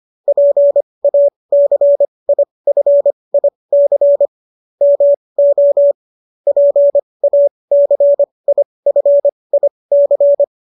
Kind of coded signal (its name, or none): Morse code